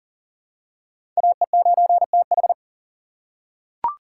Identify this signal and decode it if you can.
Morse code, heard as a keyed tone, about 40 wpm, 700 Hz. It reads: AE9TH
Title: USA Call Signs - 5 Characters - 2x Character Spacing - 40wpm